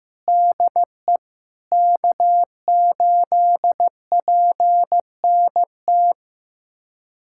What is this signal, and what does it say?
Morse code, heard as a keyed tone, about 15 wpm, 700 Hz. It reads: DE K8PNT
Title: USA Callsign DE Callsign - 5 Characters - 15wpm